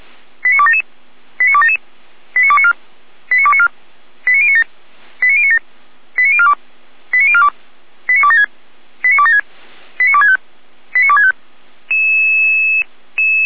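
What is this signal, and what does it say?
Five-tone selective call (ZVEI), heard as quick runs of five tones, tones of about 70 ms. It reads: 89277 89277 89294 89294 89097 89097 89042 89042 89276 89276 89265 89265
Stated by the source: ZVEI five-tone call 89310 received by SDR, fire brigade Feuerwehr Orlamünde.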